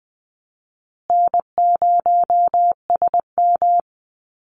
Morse code, heard as a keyed tone, about 20 wpm, 700 Hz. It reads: N0SM
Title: Call Signs Repeated 2x - 20wpm